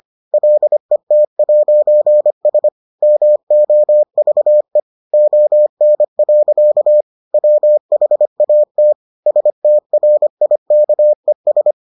Morse code, heard as a keyed tone, about 25 wpm, 600 Hz. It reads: LET'S MOVE ON. WHAT STRIKES